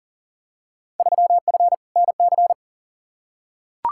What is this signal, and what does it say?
Morse code, heard as a keyed tone, about 40 wpm, 700 Hz. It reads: 3F NC
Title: Field Day Exchanges - 40wpm